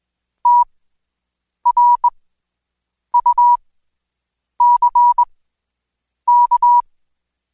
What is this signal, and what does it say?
Morse code, heard as a keyed tone, about 21 wpm, spaced as for 7 wpm, 970 Hz. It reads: TRUCK
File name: Code Quick / AudioDownld2 / Seg112-Day17-DayDrill 17